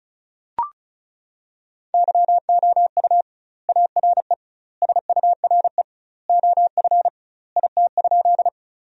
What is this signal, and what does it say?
Morse code, heard as a keyed tone, about 36 wpm, 700 Hz. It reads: YOU ARE SURE OF IT?
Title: 2x Word-Spacing - Sentences from Top 300 Words - 35wpm